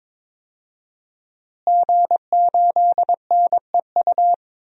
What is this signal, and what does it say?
Morse code, heard as a keyed tone, about 22 wpm, 700 Hz. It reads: G8NEU
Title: DXCC-100 Call Signs - 5 Characters - Repeated 2x - 22wpm